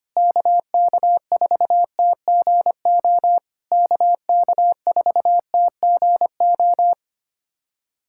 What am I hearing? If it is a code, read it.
Morse code, heard as a keyed tone, about 25 wpm, 690 Hz. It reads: KK4TGO KK4TGO
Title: SOTA QSO Practice 25wpm